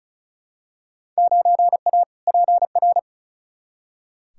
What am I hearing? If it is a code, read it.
Morse code, heard as a keyed tone, about 36 wpm, 700 Hz. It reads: 9A PR
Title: Field Day Exchanges Encore - 35wpm